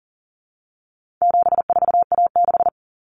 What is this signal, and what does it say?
Morse code, heard as a keyed tone, about 40 wpm, 700 Hz. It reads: 74A6
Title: Common Cut Numbers - 4-Digit 40wpm